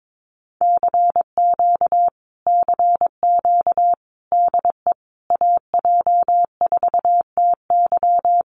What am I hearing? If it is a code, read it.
Morse code, heard as a keyed tone, about 22 wpm, 700 Hz. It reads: CQ CQ DE AJ4TY
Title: Common QSO Phrases Encore 22wpm